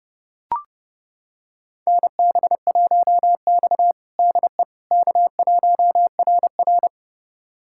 Morse code, heard as a keyed tone, about 30 wpm, 700 Hz. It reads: NB1X DE K1RR
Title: USA Callsign DE Callsign - 4 Characters - 30wpm